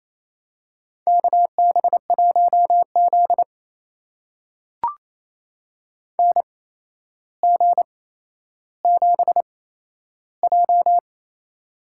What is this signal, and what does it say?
Morse code, heard as a keyed tone, about 28 wpm, spaced as for 12 wpm, 700 Hz. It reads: KB1Z NG7J
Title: USA Call Signs - 8x Character Spacing - 28wpm